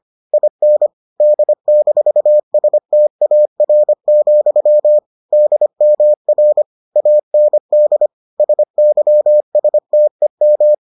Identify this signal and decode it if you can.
Morse code, heard as a keyed tone, about 25 wpm, 600 Hz. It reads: IN D-STAR, DMR AND SYSTEM